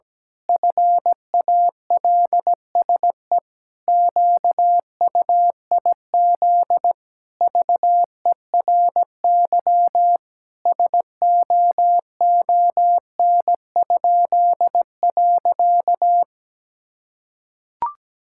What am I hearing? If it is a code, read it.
Morse code, heard as a keyed tone, about 17 wpm, 700 Hz. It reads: FALSE QUIZ VERY SOON?.
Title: Pangram Challenge v2 - 17wpm